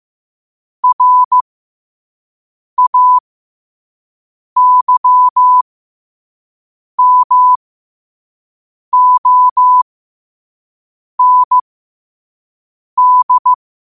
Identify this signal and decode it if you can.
Morse code, heard as a keyed tone, about 15 wpm, spaced as for 5 wpm, 1000 Hz. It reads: RAYMOND
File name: qotdfiles / 5wpm